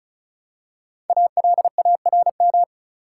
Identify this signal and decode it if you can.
Morse code, heard as a keyed tone, about 35 wpm, 700 Hz. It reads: ALARM
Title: Five Letter Words - 35wpm